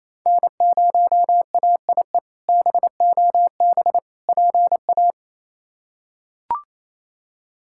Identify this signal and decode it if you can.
Morse code, heard as a keyed tone, about 28 wpm, 700 Hz. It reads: N0AIE BOB PA
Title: K1USN SST QSO Practice - 28wpm